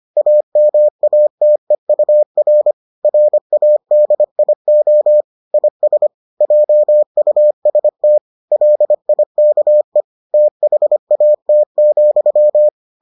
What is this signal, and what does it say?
Morse code, heard as a keyed tone, about 25 wpm, 600 Hz. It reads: AMATEUR RADIO IS JUST LIKE THAT,